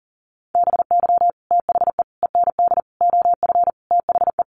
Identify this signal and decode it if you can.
Morse code, heard as a keyed tone, about 40 wpm, 700 Hz. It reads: BY THE END OF THE
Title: Sets of 5 Words 40wpm